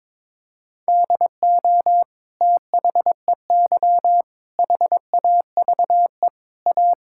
Morse code, heard as a keyed tone, about 22 wpm, 700 Hz. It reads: DO THEY HAVE A
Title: Sentences from Top 1000 Words - Review Part 6 - 22wpm